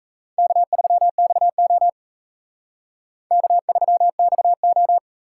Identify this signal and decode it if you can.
Morse code, heard as a keyed tone, about 34 wpm, 700 Hz. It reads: K3XO K3XO
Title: Speed Racing - USA 1x2 Call Signs 28wpm